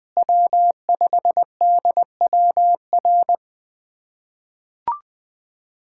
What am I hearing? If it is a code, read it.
Morse code, heard as a keyed tone, about 21 wpm, 700 Hz. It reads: W5DWR
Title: USA Callsign DE Callsign - 5 Characters - 20wpm